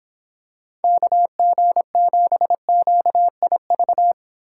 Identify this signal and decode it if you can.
Morse code, heard as a keyed tone, about 26 wpm, 700 Hz. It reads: KG7QIV